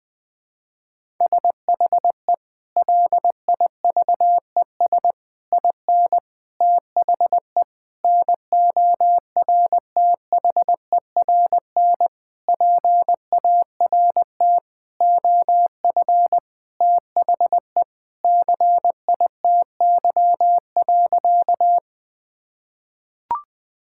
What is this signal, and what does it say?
Morse code, heard as a keyed tone, about 20 wpm, 700 Hz. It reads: SHE LIVES IN THE NORTHERN PART OF THE CITY.
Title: Sentences with 6-Word Phrases - 20wpm